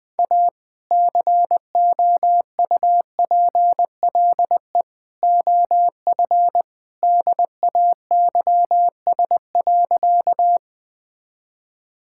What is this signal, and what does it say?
Morse code, heard as a keyed tone, about 20 wpm, 700 Hz. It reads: A COUPLE OF DAYS.
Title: Sentences with 2-Word Phrases - 20wpm